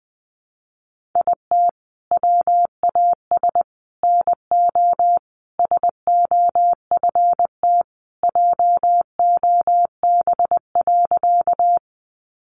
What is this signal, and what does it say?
Morse code, heard as a keyed tone, about 20 wpm, 700 Hz. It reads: IT WAS NO SOFT JOB.
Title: Sentences from Top 1500 Words 20wpm